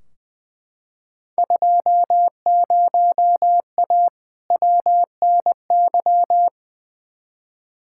Morse code, heard as a keyed tone, about 20 wpm, 700 Hz. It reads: 20A WNY
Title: Field Day Exchanges Encore - 20wpm